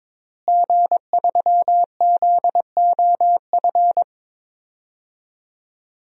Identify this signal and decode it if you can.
Morse code, heard as a keyed tone, about 22 wpm, 700 Hz. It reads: G3ZOF
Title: DXCC-100 Call Signs - 5 Characters - Repeated 2x - 22wpm